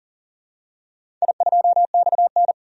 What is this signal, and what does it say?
Morse code, heard as a keyed tone, about 40 wpm, 700 Hz. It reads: I2XN
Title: DXCC-100 Call Signs - 4 Characters - 40wpm